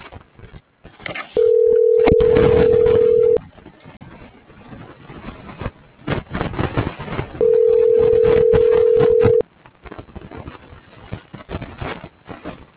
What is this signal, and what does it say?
Ringback tone (North America), 440 + 480 Hz.